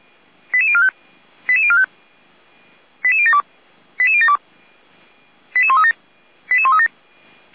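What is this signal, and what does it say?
Five-tone selective call (ZVEI), heard as quick runs of five tones, tones of about 70 ms. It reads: 88045 88045 88072 88072 88137 88137
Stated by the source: ZVEI five-tone selective call received on SDR.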